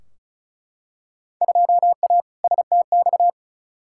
Morse code, heard as a keyed tone, about 36 wpm, 700 Hz. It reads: 2A STX
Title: Field Day Exchanges - 35wpm